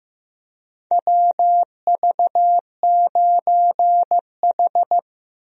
Morse code, heard as a keyed tone, about 16 wpm, 700 Hz. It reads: WV9H
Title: USA Call Signs - 4 Characters - Repeated 3x - 15wpm